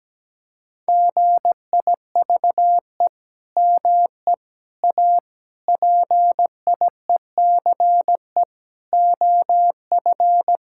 Morse code, heard as a keyed tone, about 17 wpm, 700 Hz. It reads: GIVE ME A PIECE OF
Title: Sentences from Top 1600 Words 17wpm